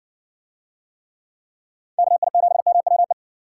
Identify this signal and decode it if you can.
Morse code, heard as a keyed tone, about 59 wpm, 700 Hz. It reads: KI7MWE